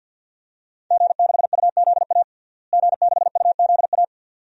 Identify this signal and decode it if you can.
Morse code, heard as a keyed tone, about 48 wpm, 700 Hz. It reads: G6UZA G6UZA
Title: DXCC-100 Call Signs - 5 Characters - Repeated 2x - 50wpm